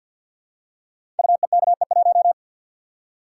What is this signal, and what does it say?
Morse code, heard as a keyed tone, about 51 wpm, 700 Hz. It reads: UEKE1
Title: Code Groups - 5 Letter:Numbers - 50wpm